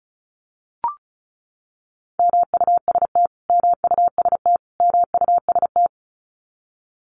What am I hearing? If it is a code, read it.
Morse code, heard as a keyed tone, about 34 wpm, 700 Hz. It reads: MUST MUST MUST
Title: Top 200 Words Repeated 35wpm